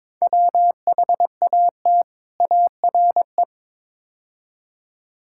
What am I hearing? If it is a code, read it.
Morse code, heard as a keyed tone, about 22 wpm, 700 Hz. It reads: WHAT ARE
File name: practice / 22wpm / Build A Phrase - 5-Word Phrases - 22wpm